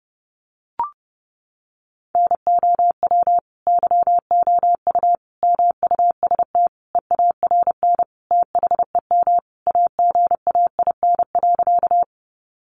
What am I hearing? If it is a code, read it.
Morse code, heard as a keyed tone, about 30 wpm, 700 Hz. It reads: NOW YOU MUST EARN THEM AGAIN.
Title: Sentences from Top 1500 Words 30wpm